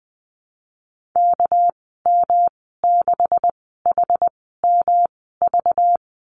Morse code, heard as a keyed tone, about 20 wpm, spaced as for 14 wpm, 700 Hz. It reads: KM6HMV
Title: 231 Lesson 37g - Letter x - Call Signs 20wpm